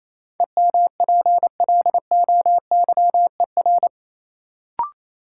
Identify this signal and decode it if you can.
Morse code, heard as a keyed tone, about 28 wpm, 700 Hz. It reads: EMPLOYER
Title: Sentences from Top 1600 Words - New Word Drill - 28wpm